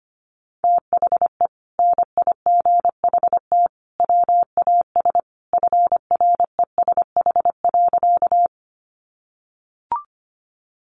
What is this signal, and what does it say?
Morse code, heard as a keyed tone, about 25 wpm, 700 Hz. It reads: THE NIGHT WAS FRESH.
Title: Sentences from Top 1200 Words 25wpm